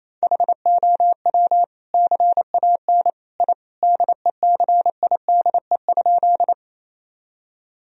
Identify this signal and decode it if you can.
Morse code, heard as a keyed tone, about 28 wpm, 700 Hz. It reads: HOW CAN I DECIDE?
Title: Sentences from Top 1000 Words - Review Part 3 - 28wpm